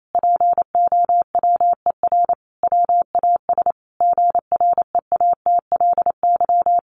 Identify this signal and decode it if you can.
Morse code, heard as a keyed tone, about 28 wpm, 700 Hz. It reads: POWER WAS GREATLY
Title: Sentences from Top 1400 Words 28wpm